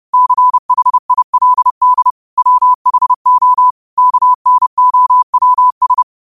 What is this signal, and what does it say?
Morse code, heard as a keyed tone, about 30 wpm, 1000 Hz. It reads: CHILD WHO KNOWS
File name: qotdfiles / 30wpm